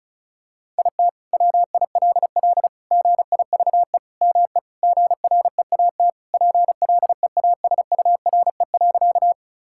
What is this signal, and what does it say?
Morse code, heard as a keyed tone, about 34 wpm, 700 Hz. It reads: IT WILL GIVE ME GREAT PLEASURE.
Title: Sentences from Top 1600 Words 35wpm